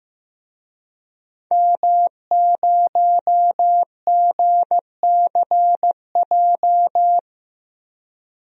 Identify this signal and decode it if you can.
Morse code, heard as a keyed tone, about 16 wpm, 700 Hz. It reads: M0GCJ
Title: DXCC-100 Call Signs - 5 Characters - Repeated 2x - 15wpm